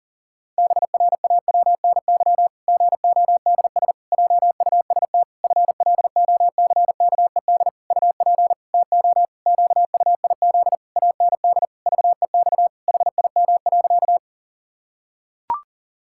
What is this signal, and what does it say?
Morse code, heard as a keyed tone, about 40 wpm, 700 Hz. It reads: BRAWNY GODS JUST FLOCKED UP TO QUIZ AND VEX HIM.
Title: Pangram Challenge v2 - 40wpm